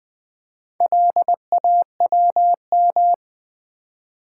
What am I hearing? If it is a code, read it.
Morse code, heard as a keyed tone, about 20 wpm, 700 Hz. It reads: LAWM
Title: Build A Word - 4 Letters At A Time - 8-Letter Words - 20wpm